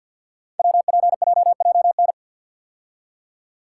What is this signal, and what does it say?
Morse code, heard as a keyed tone, about 50 wpm, 690 Hz. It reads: WPJJN
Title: Code Groups - 5 Letters - 50wpm